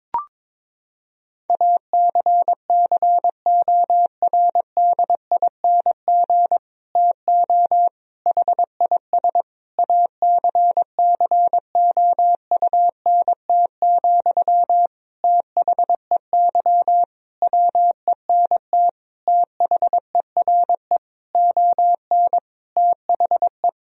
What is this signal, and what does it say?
Morse code, heard as a keyed tone, about 22 wpm, 700 Hz. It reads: ACCORDING TO HIS ACCOUNT, THEY WENT THERE ON THE
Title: Sentences from Top 1000 Words - Review Part 4 - 22wpm